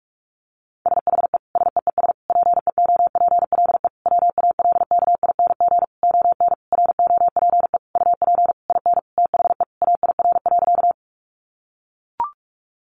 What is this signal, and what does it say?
Morse code, heard as a keyed tone, about 44 wpm, 700 Hz. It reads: SHE SEES PEOPLE WALKING ON ROPE UP IN THE AIR.
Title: Easy Phrases 45wpm